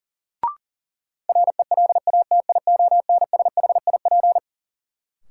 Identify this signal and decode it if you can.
Morse code, heard as a keyed tone, about 40 wpm, 700 Hz. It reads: RELATIONSHIP
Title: Sentences from Top 1100 Words - New Word Drill - 40wpm